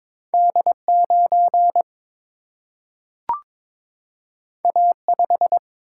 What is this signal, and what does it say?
Morse code, heard as a keyed tone, about 22 wpm, 700 Hz. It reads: D 9 A 5
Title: Single-Letter Single-Number 22wpm